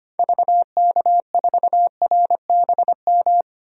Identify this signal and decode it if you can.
Morse code, heard as a keyed tone, about 25 wpm, 700 Hz. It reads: VK4RBM
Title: DXCC-100 Call Signs - 6 Characters - 25wpm